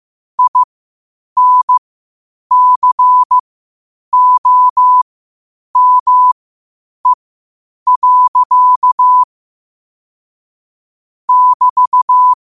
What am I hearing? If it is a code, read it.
Morse code, heard as a keyed tone, about 15 wpm, spaced as for 8 wpm, 1000 Hz. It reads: INCOME. <BT>
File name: qotdfiles / 8wpm